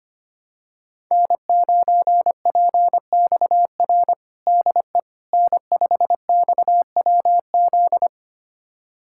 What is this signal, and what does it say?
Morse code, heard as a keyed tone, about 25 wpm, 700 Hz. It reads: N9PXR DE N5XWZ
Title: USA Callsign DE Callsign - 5 Characters - 25wpm